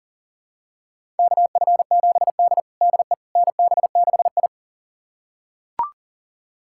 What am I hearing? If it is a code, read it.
Morse code, heard as a keyed tone, about 40 wpm, 700 Hz. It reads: KF7D DE NB6I
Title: USA Callsign DE Callsign - 4 Characters - 40wpm